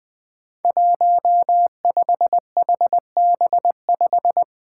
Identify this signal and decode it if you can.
Morse code, heard as a keyed tone, about 20 wpm, 700 Hz. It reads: 15HB5
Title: Five Challenging Letter-Number Combinations 20wpm